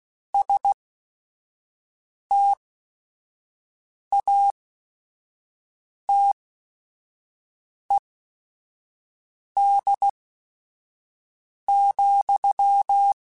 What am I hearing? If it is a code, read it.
Morse code, heard as a keyed tone, about 16 wpm, spaced as for 5 wpm, 800 Hz. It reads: STATED,
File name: Technical Subjects / Technical Subjects-mp3s / XL @5